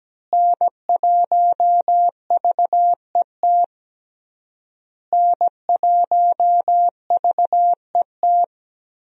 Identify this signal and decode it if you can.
Morse code, heard as a keyed tone, about 17 wpm, 700 Hz. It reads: N1VET N1VET
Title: USA Call Signs - 5 Characters - Repeated 3x - 17wpm